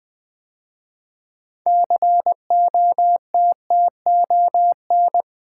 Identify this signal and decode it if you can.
Morse code, heard as a keyed tone, about 20 wpm, 700 Hz. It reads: COTTON
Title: 6 Letter Words - First 3 Letters Given - 20wpm